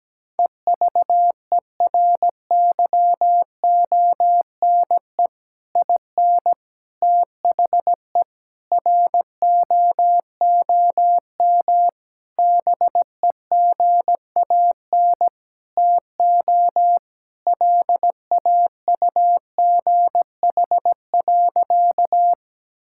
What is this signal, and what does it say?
Morse code, heard as a keyed tone, about 17 wpm, 700 Hz. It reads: EVERYONE IN THE ROOM BEGAN TO LAUGH.
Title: Sentences from Top 1000 Words - Review Part 5 - 17wpm